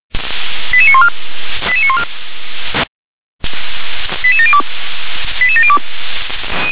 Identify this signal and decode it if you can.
Five-tone selective call (ZVEI), heard as quick runs of five tones, tones of about 70 ms. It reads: 88914 88914 88702 88702